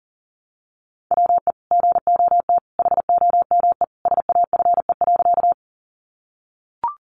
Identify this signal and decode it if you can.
Morse code, heard as a keyed tone, about 40 wpm, 700 Hz. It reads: WE GOT HOME SAFE.
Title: Sentences from Top 900 Words 40wpm